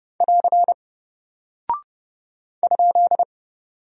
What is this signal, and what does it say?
Morse code, heard as a keyed tone, about 30 wpm, 700 Hz. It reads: <AR> ?
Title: Single Character (Letter:Number:Punctuation:Prosign) Alt 30wpm